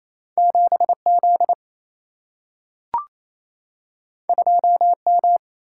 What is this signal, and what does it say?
Morse code, heard as a keyed tone, about 28 wpm, 700 Hz. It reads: 7 Z 2 M
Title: Single-Letter Single-Number 28wpm